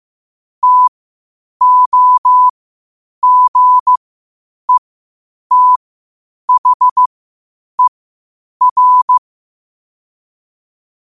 Morse code, heard as a keyed tone, about 15 wpm, spaced as for 8 wpm, 1000 Hz. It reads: TOGETHER
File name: qotdfiles / 8wpm